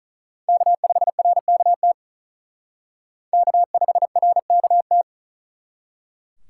Morse code, heard as a keyed tone, about 45 wpm, 700 Hz. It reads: K5RKT K5RKT